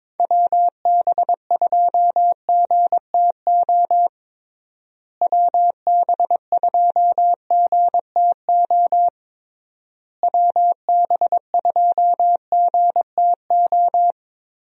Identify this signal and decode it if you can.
Morse code, heard as a keyed tone, about 22 wpm, 700 Hz. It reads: WB2GTO WB2GTO WB2GTO
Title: USA Call Signs - 6 Characters - Repeated 3x - 22wpm